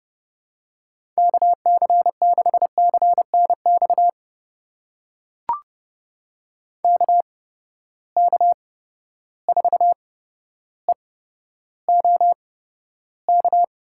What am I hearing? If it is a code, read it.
Morse code, heard as a keyed tone, about 30 wpm, spaced as for 13 wpm, 700 Hz. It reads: KC6CNX KK4EOK
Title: USA Call Signs - 6 Characters - 8x Character Spacing - 30wpm